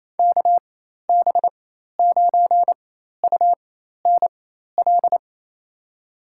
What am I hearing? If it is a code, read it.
Morse code, heard as a keyed tone, about 28 wpm, spaced as for 13 wpm, 700 Hz. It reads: KB9UNL
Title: USA Call Signs - 6 Characters - 4x Character Spacing - 28wpm